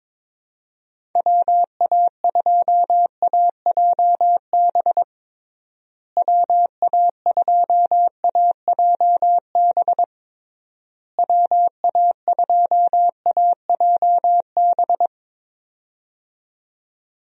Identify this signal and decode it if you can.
Morse code, heard as a keyed tone, about 22 wpm, 700 Hz. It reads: WA2AJB WA2AJB WA2AJB
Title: USA Call Signs - 6 Characters - Repeated 3x - 22wpm